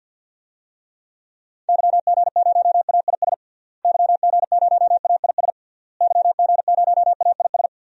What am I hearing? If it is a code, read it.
Morse code, heard as a keyed tone, about 50 wpm, 700 Hz. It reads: YG0AIS YG0AIS YG0AIS
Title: DXCC-100 Call Signs - 6 Characters - Repeated 3x - 50wpm